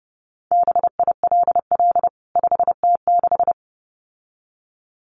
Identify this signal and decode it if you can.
Morse code, heard as a keyed tone, about 28 wpm, 700 Hz. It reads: BILL 5T6
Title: CWT Exchange 30wpm